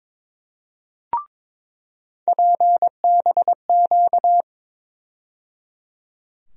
Morse code, heard as a keyed tone, about 22 wpm, 700 Hz. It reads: PBQ
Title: USA Call Sign Postfixes - 3 Letters 22wpm